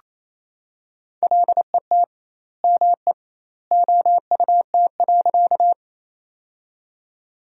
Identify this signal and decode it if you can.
Morse code, heard as a keyed tone, about 28 wpm, 700 Hz. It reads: LET ME OUT.
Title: 2x Word-Spacing - Sentences from Top 300 Words - 28wpm